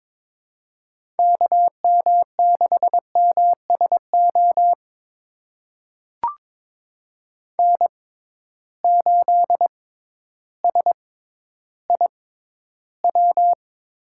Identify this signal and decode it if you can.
Morse code, heard as a keyed tone, about 22 wpm, spaced as for 12 wpm, 700 Hz. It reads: KM6MSO N8SIW
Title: USA Call Signs - 6x Character Spacing - 22wpm